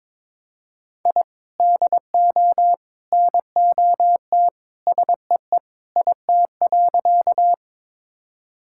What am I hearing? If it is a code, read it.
Morse code, heard as a keyed tone, about 22 wpm, 700 Hz. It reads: I DO NOT SEE IT.
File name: practice / 22wpm / Sentences with 3-Word Phrases - 22wpm